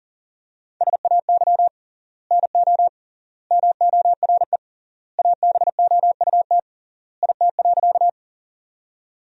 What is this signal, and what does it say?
Morse code, heard as a keyed tone, about 40 wpm, 700 Hz. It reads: SAY NO MORE ABOUT IT.
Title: 3x Word-Spacing - Sentences from Top 200 Words - 40wpm